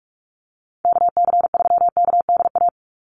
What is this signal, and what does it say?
Morse code, heard as a keyed tone, about 44 wpm, 700 Hz. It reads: KC3KDA
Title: USA Call Signs 45wpm